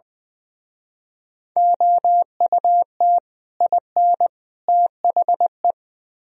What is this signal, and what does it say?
Morse code, heard as a keyed tone, about 24 wpm, 700 Hz. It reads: OUT IN THE
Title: Speed-Racing - Sets of 3 Words 20wpm